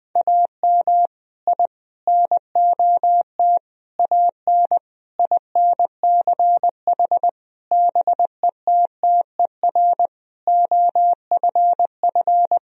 Morse code, heard as a keyed tone, about 20 wpm, 700 Hz. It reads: AM I NOT AN INCH BETTER OFF
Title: Sentences from Top 1000 Words - Review Part 3 - 20wpm